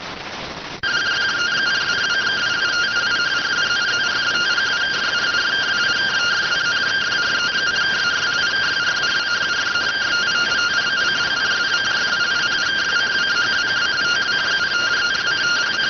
Two-tone teleprinter signal, about 170 Hz shift.